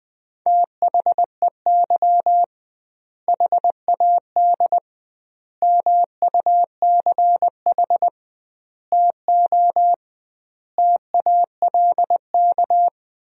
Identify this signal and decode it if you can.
Morse code, heard as a keyed tone, about 20 wpm, 700 Hz. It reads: THEY HAD MUCH TO TALK
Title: 2x Word-Spacing - Sentences from Top 400 Words - 20wpm